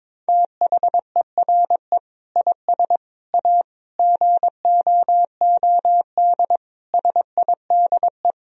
Morse code, heard as a keyed tone, about 21 wpm, 700 Hz. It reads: THERE IS A GOOD SIDE
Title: Sentences from Top 1000 Words - Review Part 1 - 22wpm